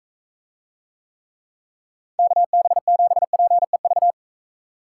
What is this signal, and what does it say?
Morse code, heard as a keyed tone, about 40 wpm, 700 Hz. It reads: KB7PEV